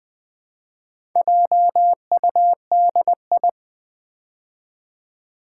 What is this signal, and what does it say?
Morse code, heard as a keyed tone, about 20 wpm, 700 Hz. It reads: JUDI
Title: Build A Word - 4 Letters At A Time - 8-Letter Words - 20wpm